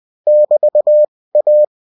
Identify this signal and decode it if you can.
Morse code, heard as a keyed tone, about 20 wpm, 600 Hz. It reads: <BT> A